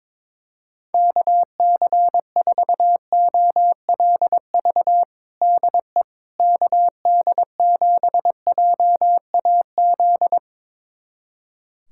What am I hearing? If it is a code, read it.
Morse code, heard as a keyed tone, about 21 wpm, 700 Hz. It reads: KC4OLV DE KD7JAZ
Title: USA Callsign DE Callsign - 6 Characters - 22wpm